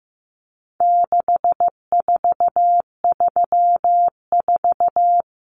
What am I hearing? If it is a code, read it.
Morse code, heard as a keyed tone, about 15 wpm, 700 Hz. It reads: 6434
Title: Numbers - 4-Digit 15wpm